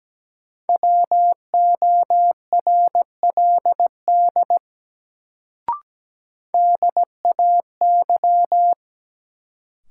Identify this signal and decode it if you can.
Morse code, heard as a keyed tone, about 17 wpm, 700 Hz. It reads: WORLD DAY
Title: Sentences from Top 200 Words - New Word Drill - 17wpm